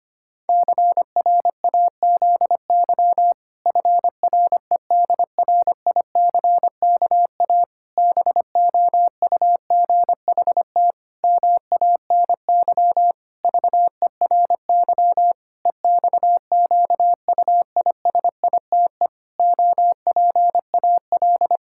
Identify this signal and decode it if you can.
Morse code, heard as a keyed tone, about 25 wpm, 700 Hz. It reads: CRAZY FREDRICKA BOUGHT MANY VERY EXQUISITE OPAL
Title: Pangram Challenge v2 - 25wpm